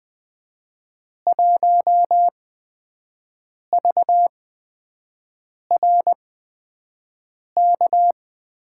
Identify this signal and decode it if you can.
Morse code, heard as a keyed tone, about 20 wpm, spaced as for 5 wpm, 700 Hz. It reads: 1VRK
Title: USA Call Signs - 5 Characters - 8x Character Spacing - 20wpm